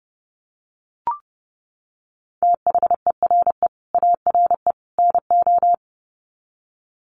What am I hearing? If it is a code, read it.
Morse code, heard as a keyed tone, about 30 wpm, 700 Hz. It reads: THERE ARE NO
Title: Sets of 3 Words 30wpm